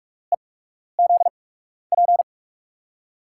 Morse code, heard as a keyed tone, about 42 wpm, spaced as for 12 wpm, 700 Hz. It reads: EZP
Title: USA Call Signs - 8x Character Spacing - 45wpm